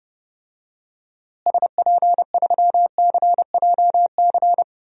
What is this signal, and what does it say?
Morse code, heard as a keyed tone, about 30 wpm, 700 Hz. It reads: SP3CJC
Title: DXCC-100 Call Signs - 6 Characters - Repeated 3x - 30wpm